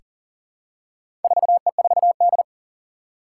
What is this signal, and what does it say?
Morse code, heard as a keyed tone, about 40 wpm, 700 Hz. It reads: 4E4D
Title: Four Challenging Letter-Number Combinations 40wpm